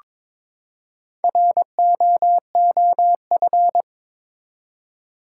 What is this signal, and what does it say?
Morse code, heard as a keyed tone, about 21 wpm, 700 Hz. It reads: ROOF
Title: Four Letter Words - 22wpm